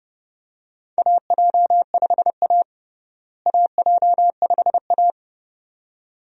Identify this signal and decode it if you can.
Morse code, heard as a keyed tone, about 30 wpm, 700 Hz. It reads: AJ5A AJ5A
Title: USA Call Signs - 4 Characters - Repeated 2x - 30wpm